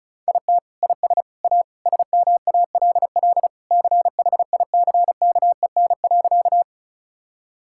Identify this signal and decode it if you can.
Morse code, heard as a keyed tone, about 34 wpm, 700 Hz. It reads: IT IS A SMALL CHICKEN.
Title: Sentences from Top 1000 Words - Review Part 5 - 35wpm